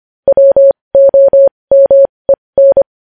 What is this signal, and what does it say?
Morse code, heard as a keyed tone, about 25 wpm, spaced as for 20 wpm, 550 Hz. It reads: WOMEN
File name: podcast / News headlines-1x-25x20